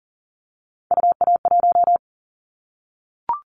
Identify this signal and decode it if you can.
Morse code, heard as a keyed tone, about 40 wpm, 700 Hz. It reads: UA1
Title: Callsign Prefixes for Top 50 DXCC - 40wpm